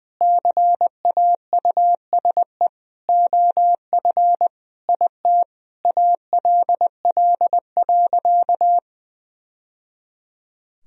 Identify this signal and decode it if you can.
Morse code, heard as a keyed tone, about 20 wpm, 700 Hz. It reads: CAUSE OF IT ALL.
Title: Sentences from Top 1000 Words - Review Part 1 - 20wpm